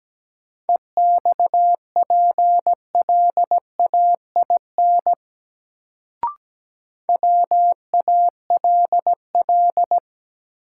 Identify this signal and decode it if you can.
Morse code, heard as a keyed tone, about 17 wpm, 700 Hz. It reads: EXPLAIN WALL
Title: Sentences from Top 1100 Words - New Word Drill Alt - 17wpm